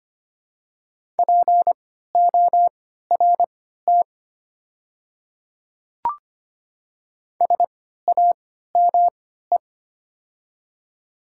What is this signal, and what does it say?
Morse code, heard as a keyed tone, about 25 wpm, spaced as for 14 wpm, 700 Hz. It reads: PORT SAME
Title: 3x Character Spacing - Top 200 Words - 25wpm